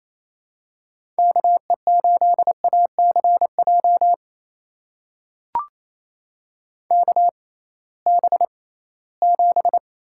Morse code, heard as a keyed tone, about 28 wpm, spaced as for 18 wpm, 700 Hz. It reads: KE8ACJ KB7
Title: USA Call Signs - 6 Characters - 6x Character Spacing - 28wpm